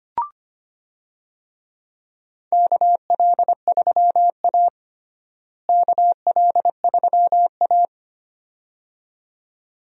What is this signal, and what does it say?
Morse code, heard as a keyed tone, about 25 wpm, 700 Hz. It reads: KL3A KL3A
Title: USA Call Signs - 4 Characters - Repeated 2x - 25wpm